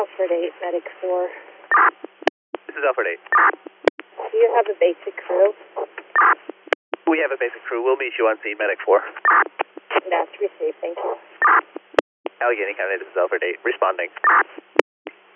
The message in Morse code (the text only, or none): none